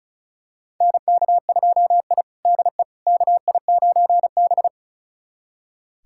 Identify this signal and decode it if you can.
Morse code, heard as a keyed tone, about 36 wpm, 700 Hz. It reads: NK2I DE KI9B
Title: USA Callsign DE Callsign - 4 Characters - 35wpm